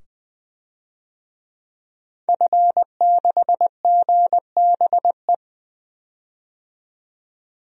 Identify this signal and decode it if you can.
Morse code, heard as a keyed tone, about 20 wpm, 700 Hz. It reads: F6GBE
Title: DXCC-100 Call Signs - 5 Characters - 20wpm